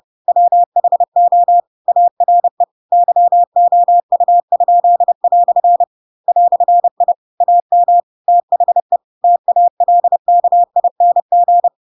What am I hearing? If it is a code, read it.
Morse code, heard as a keyed tone, about 30 wpm, 700 Hz. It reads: WHO ARE YOU?" "I AM THE TALKING
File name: Pinocchio / Pinocchio ch4 30 30 ik1hge